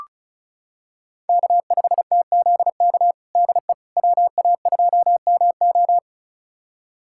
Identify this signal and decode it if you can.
Morse code, heard as a keyed tone, about 36 wpm, 700 Hz. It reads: K5TZK DE WA2MO
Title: USA Callsign DE Callsign - 5 Characters - 35wpm